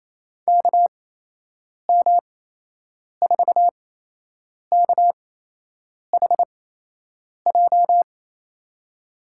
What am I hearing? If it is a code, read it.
Morse code, heard as a keyed tone, about 28 wpm, spaced as for 8 wpm, 700 Hz. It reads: KM4KHJ
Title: USA Call Signs - 6 Characters - 8x Character Spacing - 28wpm